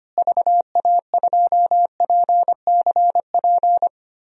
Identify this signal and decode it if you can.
Morse code, heard as a keyed tone, about 25 wpm, 700 Hz. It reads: VA2PCP
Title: DXCC-100 Call Signs - 6 Characters - Repeated 3x - 25wpm